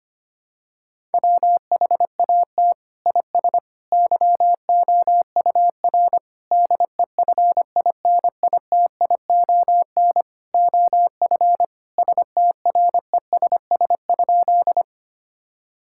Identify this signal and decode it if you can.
Morse code, heard as a keyed tone, about 25 wpm, 700 Hz. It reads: WHAT IS YOUR DEFINITION OF STRESS?
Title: Sentences from Top 1000 Words - Review Part 6 - 25wpm